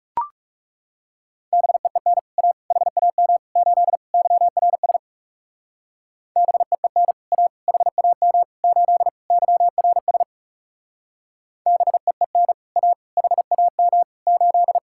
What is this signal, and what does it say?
Morse code, heard as a keyed tone, about 40 wpm, 700 Hz. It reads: BEEN A HAM 8 YRS BEEN A HAM 8 YRS BEEN A HAM 8